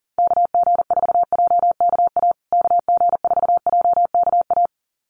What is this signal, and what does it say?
Morse code, heard as a keyed tone, about 40 wpm, 700 Hz. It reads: KG4JKA KG4JKA
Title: USA Call Signs Repeated 2x 40wpm